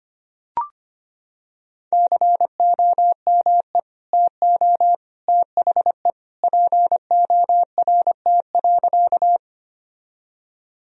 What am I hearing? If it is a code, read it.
Morse code, heard as a keyed tone, about 25 wpm, 700 Hz. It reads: COME TO THE PORT.
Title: Sentences from Top 1000 Words - Review Part 1 - 25wpm